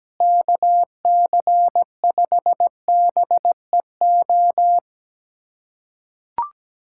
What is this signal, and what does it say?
Morse code, heard as a keyed tone, about 17 wpm, 700 Hz. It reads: KC5BEO
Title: USA Callsign DE Callsign - 6 Characters - 17wpm